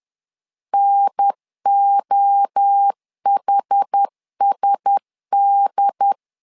Morse code, heard as a keyed tone, about 10 wpm, 800 Hz. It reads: NOHSD